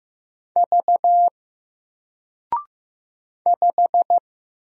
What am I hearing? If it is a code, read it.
Morse code, heard as a keyed tone, about 15 wpm, 700 Hz. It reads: V 5
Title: Single Letter-Number 15wpm